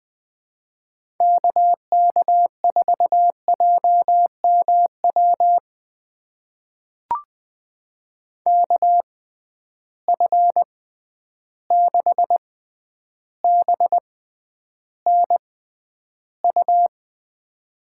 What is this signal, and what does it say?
Morse code, heard as a keyed tone, about 20 wpm, spaced as for 10 wpm, 700 Hz. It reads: KK4JMW KF6BNU
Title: USA Call Signs - 6 Characters - 6x Character Spacing - 20wpm